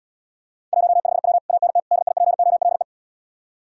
Morse code, heard as a keyed tone, about 80 wpm, 700 Hz. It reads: 15W ANT DIPOLE